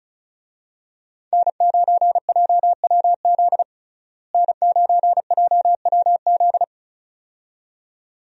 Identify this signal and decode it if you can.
Morse code, heard as a keyed tone, about 36 wpm, 700 Hz. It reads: N9JWZ N9JWZ
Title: USA Call Signs - 5 Characters - Repeated 2x - 35wpm